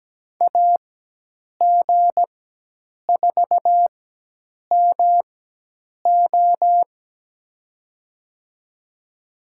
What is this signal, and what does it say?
Morse code, heard as a keyed tone, about 17 wpm, spaced as for 8 wpm, 700 Hz. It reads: AG4MO
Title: USA Call Signs - 5 Characters - 4x Character Spacing - 17wpm